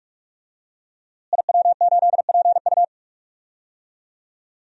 Morse code, heard as a keyed tone, about 44 wpm, 700 Hz. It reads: IW8PU
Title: DXCC-100 Call Signs - 5 Characters - Repeated 2x - 45wpm